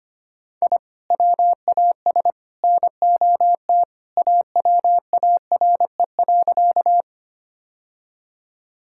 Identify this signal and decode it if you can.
Morse code, heard as a keyed tone, about 25 wpm, 700 Hz. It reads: I WAS NOT AWARE.
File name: practice / 25wpm / Sentences with 3-Word Phrases - 25wpm